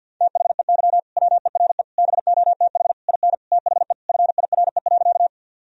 Morse code, heard as a keyed tone, about 46 wpm, 700 Hz. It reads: THEY WERE BOTH IN THE FIRE.
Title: Sentences from Top 1000 Words - Review Part 1 - 50wpm